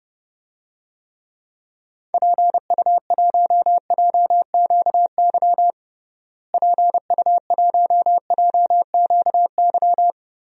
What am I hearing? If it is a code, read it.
Morse code, heard as a keyed tone, about 32 wpm, 700 Hz. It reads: PU1JQY PU1JQY
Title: DXCC-100 Call Signs - 6 Characters - Repeated 3x - 30wpm